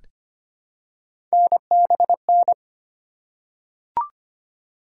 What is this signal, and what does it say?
Morse code, heard as a keyed tone, about 25 wpm, 700 Hz. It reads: NBN
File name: Three Letter-Number Combinations - 25wpm